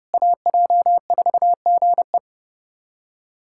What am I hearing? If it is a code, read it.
Morse code, heard as a keyed tone, about 30 wpm, 690 Hz. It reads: AJ4GE
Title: SOTA QSO Practice 30wpm